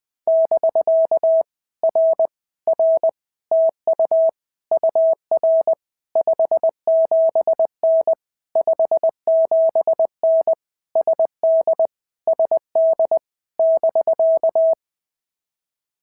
Morse code, heard as a keyed tone, about 20 wpm, 650 Hz. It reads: <BK> R R TU UR 57N 57N SD SD <BK>
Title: POTA QSO Practice - 20wpm